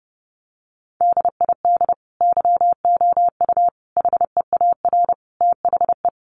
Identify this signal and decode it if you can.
Morse code, heard as a keyed tone, about 30 wpm, 700 Hz. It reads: DID YOU HEAR THE
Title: Sentences from Top 1400 Words 30wpm